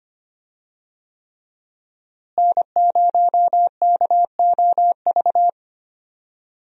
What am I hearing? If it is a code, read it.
Morse code, heard as a keyed tone, about 25 wpm, 700 Hz. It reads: N0KOV
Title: USA Call Signs - 5 Characters - Repeated 3x - 25wpm